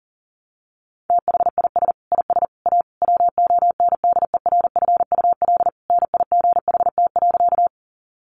Morse code, heard as a keyed tone, about 40 wpm, 700 Hz. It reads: THIS IS A WONDERFUL NIGHT.
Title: Sentences from Top 1100 Words 40wpm